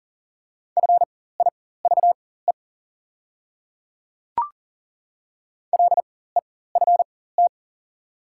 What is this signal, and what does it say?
Morse code, heard as a keyed tone, about 41 wpm, spaced as for 19 wpm, 690 Hz. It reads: FIVE LEFT
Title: 4x Character Spacing - Top 500 Words - 40wpm